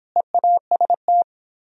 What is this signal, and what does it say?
Morse code, heard as a keyed tone, about 31 wpm, 700 Hz. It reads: EAST